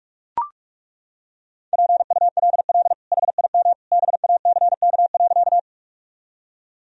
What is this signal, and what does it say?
Morse code, heard as a keyed tone, about 41 wpm, 700 Hz. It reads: PULL HIM BACK.
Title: Sentences from Top 1000 Words - Review Part 2 - 45wpm